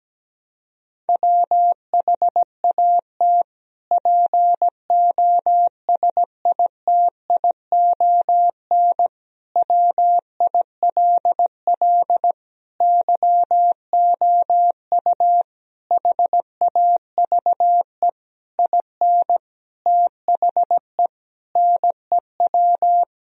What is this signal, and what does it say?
Morse code, heard as a keyed tone, about 17 wpm, 700 Hz. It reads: WHAT POSITION WILL YOU HAVE IN THE NEW
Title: Sentences from Top 1000 Words - Review Part 3 - 17wpm